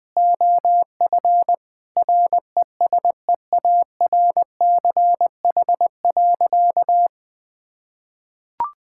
Morse code, heard as a keyed tone, about 21 wpm, 700 Hz. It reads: OF RESEARCH.
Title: Sentences from Top 1000 Words - Review Part 5 - 20wpm